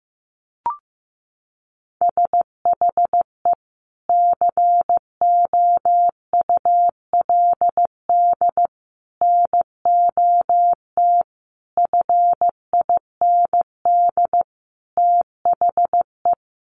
Morse code, heard as a keyed tone, about 15 wpm, 700 Hz. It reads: SHE COULD NOT FIND THE
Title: Sentences from Top 1100 Words 15wpm